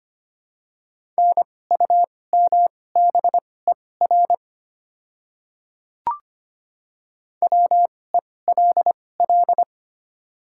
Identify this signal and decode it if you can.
Morse code, heard as a keyed tone, about 25 wpm, spaced as for 18 wpm, 700 Hz. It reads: NUMBER WELL
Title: 2x Character Spacing - Top 500 Words - 25wpm